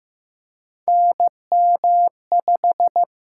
Morse code, heard as a keyed tone, about 15 wpm, 700 Hz. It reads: NM5
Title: USA Call Signs - 4 Characters - 6x Character Spacing - 15wpm